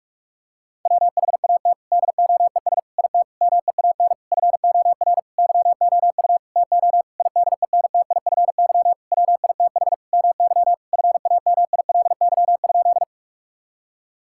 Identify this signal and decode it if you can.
Morse code, heard as a keyed tone, about 46 wpm, 700 Hz. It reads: WHAT DOES IT MEAN FOR YOU TO IDENTIFY WITH MY FAMILY?
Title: Sentences from Top 1000 Words - Review Part 6 - 45wpm